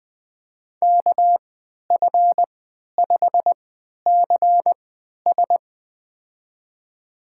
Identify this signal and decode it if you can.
Morse code, heard as a keyed tone, about 20 wpm, spaced as for 11 wpm, 700 Hz. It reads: KF5CS
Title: USA Call Signs - 3x Character Spacing - 20wpm